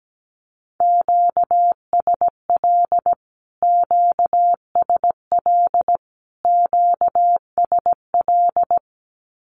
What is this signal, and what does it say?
Morse code, heard as a keyed tone, about 17 wpm, 700 Hz. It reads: QSL QSL QSL
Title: Q-Codes Repeated 3x 17wpm